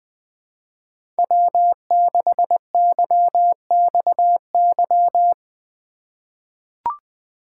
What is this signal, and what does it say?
Morse code, heard as a keyed tone, about 20 wpm, 700 Hz. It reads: W6YXY
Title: USA Call Signs - 5 Characters - 4x Character Spacing - 20wpm